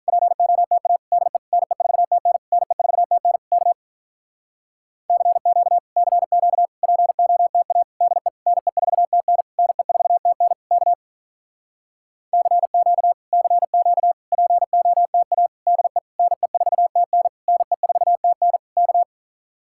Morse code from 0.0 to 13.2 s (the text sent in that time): POTA DE NE4TN NE4TN K CQ CQ POTA DE NE4TN NE4TN K CQ